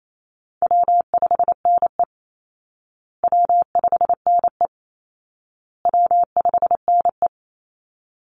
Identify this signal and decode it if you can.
Morse code, heard as a keyed tone, about 28 wpm, 700 Hz. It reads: W5NE W5NE W5NE
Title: Easy Call Signs Repeated 3x - 28wpm